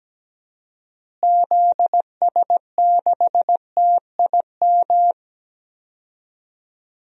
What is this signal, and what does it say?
Morse code, heard as a keyed tone, about 17 wpm, 700 Hz. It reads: ZS6TIM
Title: DXCC-100 Call Signs - 6 Characters - Repeated 2x - 17wpm